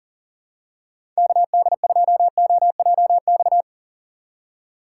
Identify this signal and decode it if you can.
Morse code, heard as a keyed tone, about 40 wpm, 700 Hz. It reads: KD2OJX
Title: USA Call Signs - 6 Characters - 3x Character Spacing - 40wpm